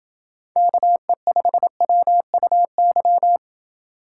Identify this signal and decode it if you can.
Morse code, heard as a keyed tone, about 27 wpm, 700 Hz. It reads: KE5WUY